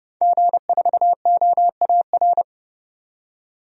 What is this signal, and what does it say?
Morse code, heard as a keyed tone, about 31 wpm, 700 Hz. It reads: G4OAR
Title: DXCC-100 Call Signs - 5 Characters - 30wpm